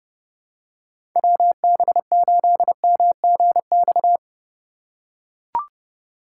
Morse code, heard as a keyed tone, about 31 wpm, 700 Hz. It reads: WB8MGX
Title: USA Call Signs - 4x Character Spacing - 30wpm